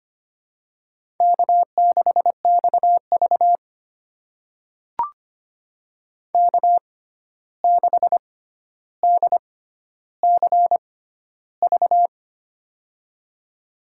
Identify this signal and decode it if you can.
Morse code, heard as a keyed tone, about 25 wpm, spaced as for 12 wpm, 700 Hz. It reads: K6XV K6DCV
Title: USA Call Signs - 6x Character Spacing - 25wpm